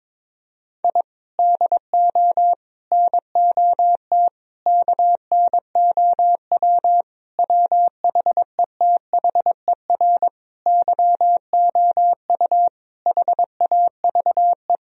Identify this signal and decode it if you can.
Morse code, heard as a keyed tone, about 22 wpm, 700 Hz. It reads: I DO NOT KNOW WHETHER YOU HAVE
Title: Sentences from Top 1000 Words - Review Part 3 - 22wpm